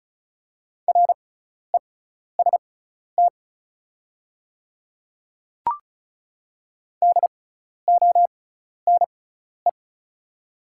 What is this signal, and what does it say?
Morse code, heard as a keyed tone, about 35 wpm, spaced as for 12 wpm, 690 Hz. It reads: REST DONE
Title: 6x Character Spacing - Top 500 Words - 35wpm